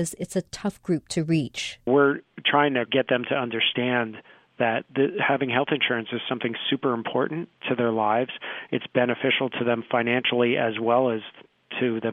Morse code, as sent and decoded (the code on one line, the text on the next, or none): none